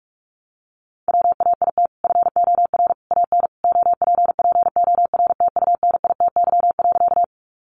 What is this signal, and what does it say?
Morse code, heard as a keyed tone, about 44 wpm, 700 Hz. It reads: WAIT FOR AN OPPORTUNITY.
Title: Sentences from Top 1000 Words 45wpm